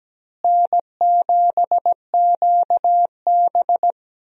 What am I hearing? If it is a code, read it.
Morse code, heard as a keyed tone, about 17 wpm, 700 Hz. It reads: N7QB
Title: USA Call Signs - 4 Characters - Repeated 3x - 17wpm